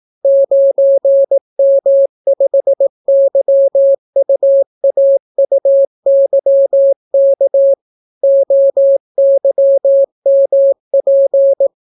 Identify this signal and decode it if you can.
Morse code, heard as a keyed tone, about 18 wpm, 550 Hz. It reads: 9M5YUAUYK OYMP